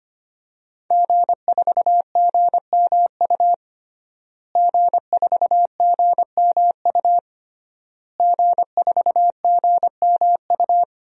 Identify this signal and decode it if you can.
Morse code, heard as a keyed tone, about 25 wpm, 700 Hz. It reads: G4GMU G4GMU G4GMU
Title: DXCC-100 Call Signs - 5 Characters - Repeated 3x - 25wpm